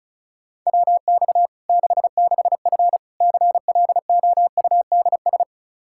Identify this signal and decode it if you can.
Morse code, heard as a keyed tone, about 34 wpm, 700 Hz. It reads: WX 66F CLOUDS
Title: Practice QSO - Exchange 2 - Info - 35wpm